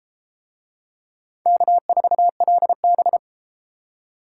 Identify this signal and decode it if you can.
Morse code, heard as a keyed tone, about 32 wpm, 700 Hz. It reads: K4LB